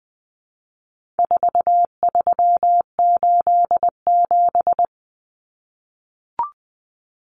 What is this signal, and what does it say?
Morse code, heard as a keyed tone, about 20 wpm, 700 Hz. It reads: 4387
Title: Numbers - 4-Digit 20wpm